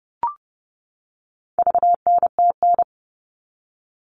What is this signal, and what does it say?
Morse code, heard as a keyed tone, about 30 wpm, 700 Hz. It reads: VNTN
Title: All Cut Numbers - 4-Digit 30wpm